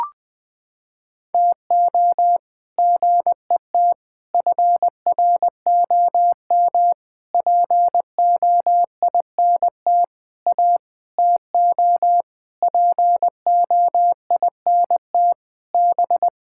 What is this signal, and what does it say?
Morse code, heard as a keyed tone, about 20 wpm, 700 Hz. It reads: TO GET FROM POINT A TO POINT B
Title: Sentences with 8-Word Phrases - 20wpm